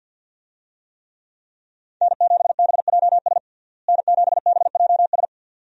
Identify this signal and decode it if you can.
Morse code, heard as a keyed tone, about 52 wpm, 700 Hz. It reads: N7BJS N7BJS
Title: USA Call Signs - 5 Characters - Repeated 2x - 50wpm